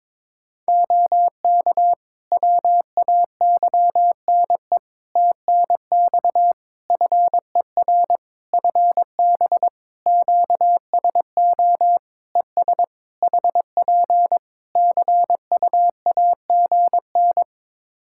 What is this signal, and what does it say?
Morse code, heard as a keyed tone, about 22 wpm, 700 Hz. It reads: OK WAYNE TNX FER FB QSO ES HP CUAGN
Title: Practice QSO - Exchange 4 - Ending - 22wpm